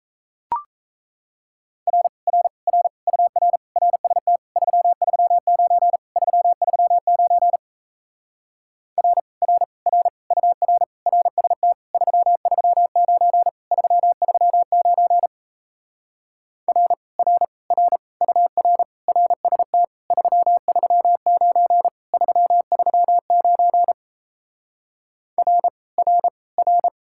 Morse code, heard as a keyed tone, about 38 wpm, 700 Hz. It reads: R R R UR RST 339 339 R R R UR RST 339 339 R R R UR RST 339 339 R R R